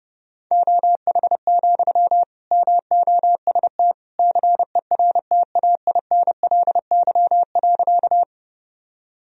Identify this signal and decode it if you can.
Morse code, heard as a keyed tone, about 30 wpm, 700 Hz. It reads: OH, MOST CERTAINLY.
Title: Sentences from Top 1000 Words - Review Part 1 - 30wpm